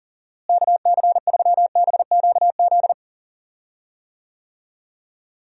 Morse code, heard as a keyed tone, about 40 wpm, 700 Hz. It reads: KC3BQZ
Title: USA Call Signs - 6 Characters - Repeated 2x - 40wpm